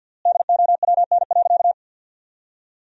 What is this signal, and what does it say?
Morse code, heard as a keyed tone, about 47 wpm, 700 Hz. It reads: DOWN.
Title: 3x Word-Spacing - Sentences from Top 500 Words - 50wpm